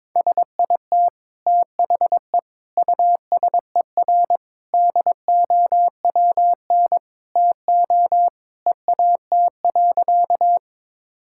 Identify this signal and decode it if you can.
Morse code, heard as a keyed tone, about 22 wpm, 700 Hz. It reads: SIT THE USER DOWN TO EAT.
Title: Sentences from Top 1000 Words - Review Part 5 - 22wpm